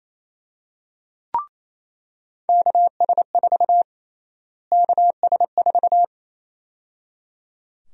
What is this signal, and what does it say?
Morse code, heard as a keyed tone, about 27 wpm, 700 Hz. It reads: KS4 KS4
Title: USA Call Sign Prefixes - Repeated 2x - 28wpm